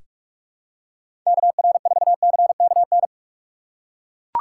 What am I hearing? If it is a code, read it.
Morse code, heard as a keyed tone, about 41 wpm, 700 Hz. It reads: KR4CKN
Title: USA Call Signs - 6 Characters - 4x Character Spacing - 45wpm